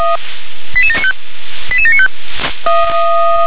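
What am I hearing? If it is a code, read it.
Five-tone selective call (ZVEI), heard as quick runs of five tones, tones of about 70 ms. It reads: 88795 88795
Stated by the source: ZVEI five-tone selective call received on an SDR.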